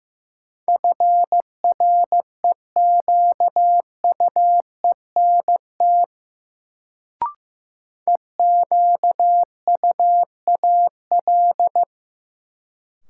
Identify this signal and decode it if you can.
Morse code, heard as a keyed tone, about 16 wpm, 700 Hz. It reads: FREQUENT EQUAL
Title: Sentences from Top 1400 Words - New Word Drill Alt - 15wpm